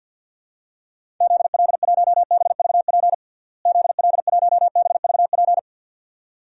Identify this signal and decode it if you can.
Morse code, heard as a keyed tone, about 48 wpm, 700 Hz. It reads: ZL1BVP ZL1BVP
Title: DXCC-100 Call Signs - 6 Characters - Repeated 2x - 50wpm